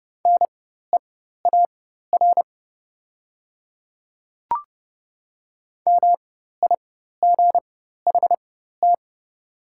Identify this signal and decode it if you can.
Morse code, heard as a keyed tone, about 30 wpm, spaced as for 14 wpm, 700 Hz. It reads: NEAR MIGHT
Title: 4x Character Spacing - Top 500 Words - 30wpm